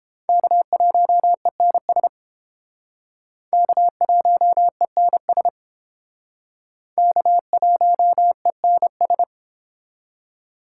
Morse code, heard as a keyed tone, about 30 wpm, 700 Hz. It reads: K1ENS K1ENS K1ENS